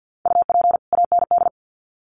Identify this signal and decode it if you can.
Morse code, heard as a keyed tone, about 50 wpm, 700 Hz. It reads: UP AND
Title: Sets of 2 Words 50wpm